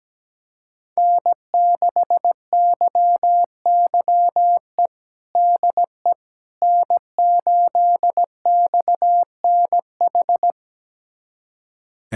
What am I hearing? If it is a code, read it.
Morse code, heard as a keyed tone, about 17 wpm, 700 Hz. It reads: N6YYE DE N8XNH
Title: USA Callsign DE Callsign - 5 Characters - 17wpm